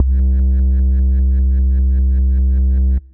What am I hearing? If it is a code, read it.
There is no code.